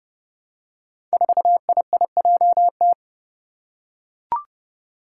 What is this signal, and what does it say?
Morse code, heard as a keyed tone, about 30 wpm, 700 Hz. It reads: 4IIJT
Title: Five Challenging Letter-Number Combinations 30wpm